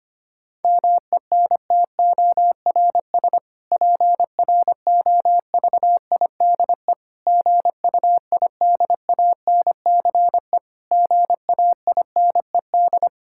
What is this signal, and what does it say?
Morse code, heard as a keyed tone, about 25 wpm, 700 Hz. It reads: MENTORS PROVIDE GUIDANCE GAINED
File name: practice / 25wpm / Difficult Sentences with Hints - 25wpm